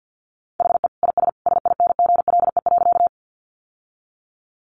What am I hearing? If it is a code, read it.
Morse code, heard as a keyed tone, about 50 wpm, 700 Hz. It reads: HE IS SINGLE.
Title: Sentences from Top 700 Words 50wpm